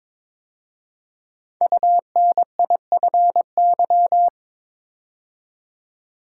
Morse code, heard as a keyed tone, about 22 wpm, 700 Hz. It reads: UNIFY
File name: 5 Letter Words - First 3 Letters Given - 22wpm